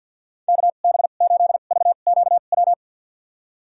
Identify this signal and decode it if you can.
Morse code, heard as a keyed tone, about 49 wpm, spaced as for 35 wpm, 700 Hz. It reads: KB8VQW
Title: USA Call Signs - 2x Character Spacing - 50wpm